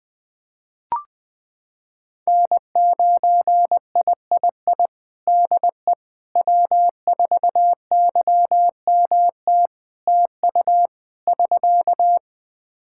Morse code, heard as a keyed tone, about 21 wpm, 700 Hz. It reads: N9III DE W4YMT TU <SK>
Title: Practice QSO - Exchange 4 - Ending - 20wpm